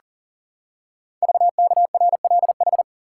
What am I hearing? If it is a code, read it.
Morse code, heard as a keyed tone, about 39 wpm, 700 Hz. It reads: VKRLH
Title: Code Groups - 5 Letters - 40wpm